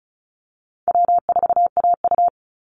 Morse code, heard as a keyed tone, about 34 wpm, 700 Hz. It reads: W4AU
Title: Call Signs - 35wpm